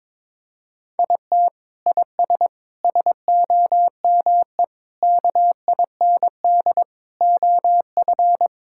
Morse code, heard as a keyed tone, about 22 wpm, 700 Hz. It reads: IT IS SOME KIND OF
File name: practice / 22wpm / Sentences with 3-Word Phrases - 22wpm